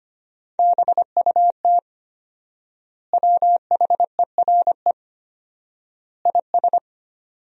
Morse code, heard as a keyed tone, about 25 wpm, 700 Hz. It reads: BUT WHERE IS
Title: 4x Word-Spacing - Sentences from Top 200 Words - 25wpm